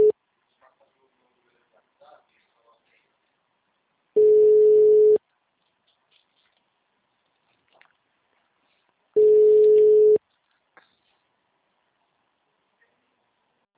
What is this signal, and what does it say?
Ringback tone (Europe, 425 Hz), with speech.
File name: out- -2002- - - .35849